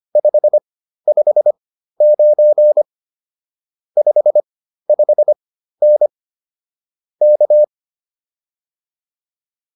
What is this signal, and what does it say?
Morse code, heard as a keyed tone, about 25 wpm, spaced as for 13 wpm, 600 Hz. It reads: 559 55N K
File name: POTA207 13